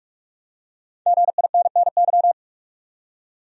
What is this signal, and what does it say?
Morse code, heard as a keyed tone, about 44 wpm, 700 Hz. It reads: GINNY
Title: USA Common Names - Max 5 Letters - 45wpm